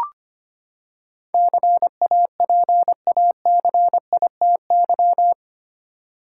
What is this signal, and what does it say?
Morse code, heard as a keyed tone, about 25 wpm, 700 Hz. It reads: CAPACITY
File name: Eight Letter Words - 25wpm